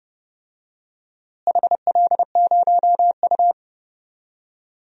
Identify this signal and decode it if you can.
Morse code, heard as a keyed tone, about 31 wpm, 700 Hz. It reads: HL0U
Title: DXCC-100 Call Signs - 4 Characters - Repeated 2x - 30wpm